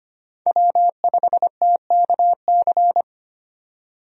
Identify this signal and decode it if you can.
Morse code, heard as a keyed tone, about 25 wpm, 700 Hz. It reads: W5TKC
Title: USA Call Signs - 5 Characters - 25wpm